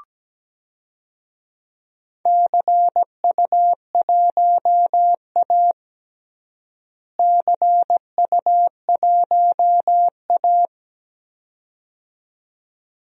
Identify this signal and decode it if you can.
Morse code, heard as a keyed tone, about 17 wpm, 700 Hz. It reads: CU1A CU1A
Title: DXCC-100 Call Signs - 4 Characters - Repeated 2x - 17wpm